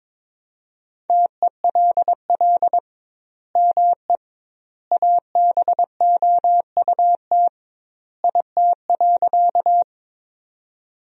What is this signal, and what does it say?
Morse code, heard as a keyed tone, about 22 wpm, 700 Hz. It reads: TELL ME ABOUT IT.
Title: 2x Word-Spacing - Sentences from Top 200 Words - 22wpm